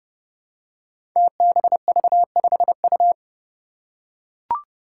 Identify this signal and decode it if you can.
Morse code, heard as a keyed tone, about 30 wpm, 700 Hz. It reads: TBV5U
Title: Five Challenging Letter-Number Combinations 30wpm